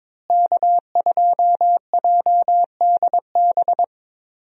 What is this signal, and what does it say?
Morse code, heard as a keyed tone, about 21 wpm, 700 Hz. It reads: K2JDB
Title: USA Call Signs - 5 Characters - Repeated 3x - 22wpm